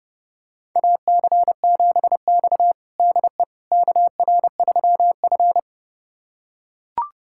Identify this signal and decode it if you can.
Morse code, heard as a keyed tone, about 32 wpm, 700 Hz. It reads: AC7X DE KR3F
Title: USA Callsign DE Callsign - 4 Characters - 30wpm